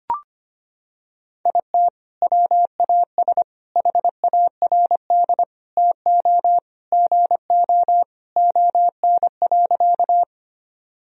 Morse code, heard as a keyed tone, about 25 wpm, 700 Hz. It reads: IT WAS HARD TO GO ON.
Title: Sentences from Top 1000 Words - Review Part 1 - 25wpm